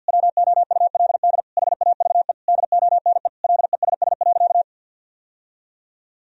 Morse code, heard as a keyed tone, about 51 wpm, 700 Hz. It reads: WOULD HAVE DONE LESS.
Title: Sentences from Top 1000 Words - Review Part 3 - 50wpm